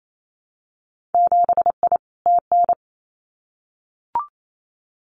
Morse code, heard as a keyed tone, about 28 wpm, 700 Hz. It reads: 7I TN
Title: Winter Field Day Exchanges Encore 28wpm